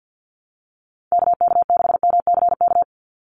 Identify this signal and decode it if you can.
Morse code, heard as a keyed tone, about 48 wpm, 700 Hz. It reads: KK6MCK
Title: USA Call Signs 50wpm